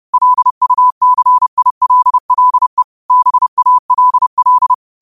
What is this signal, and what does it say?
Morse code, heard as a keyed tone, about 30 wpm, 1000 Hz. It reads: LUCILLE BALL
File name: qotdfiles / 30wpm